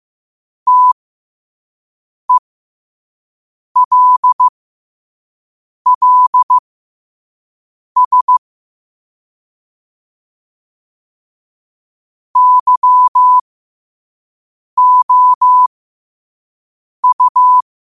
Morse code, heard as a keyed tone, about 15 wpm, spaced as for 5 wpm, 1000 Hz. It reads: TELLS YOU
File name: qotdfiles / 5wpm